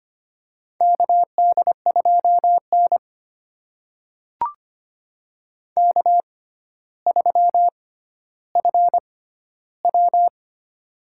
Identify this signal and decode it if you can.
Morse code, heard as a keyed tone, about 25 wpm, spaced as for 13 wpm, 700 Hz. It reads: KD2N K3FW
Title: USA Call Signs - 4 Characters - 6x Character Spacing - 25wpm